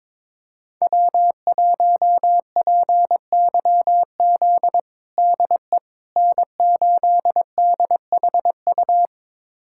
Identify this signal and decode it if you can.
Morse code, heard as a keyed tone, about 22 wpm, 700 Hz. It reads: W1PYZ DE N8DHU
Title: USA Callsign DE Callsign - 5 Characters - 22wpm